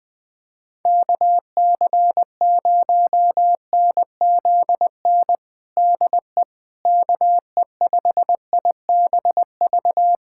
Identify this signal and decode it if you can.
Morse code, heard as a keyed tone, about 20 wpm, 700 Hz. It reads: KC0NZN DE KE5IBV
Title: USA Callsign DE Callsign 20wpm